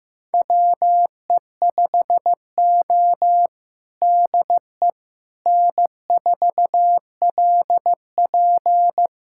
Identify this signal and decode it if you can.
Morse code, heard as a keyed tone, about 15 wpm, 700 Hz. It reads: WE5O DE N4LP
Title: USA Callsign DE Callsign - 4 Characters - 15wpm